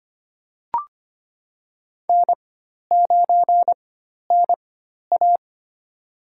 Morse code, heard as a keyed tone, about 25 wpm, spaced as for 12 wpm, 700 Hz. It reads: N9NA
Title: USA Call Signs - 4 Characters - 4x Character Spacing - 25wpm